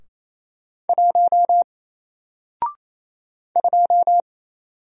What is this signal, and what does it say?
Morse code, heard as a keyed tone, about 27 wpm, 700 Hz. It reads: 1 2
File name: Single-Letter-Number-28wpm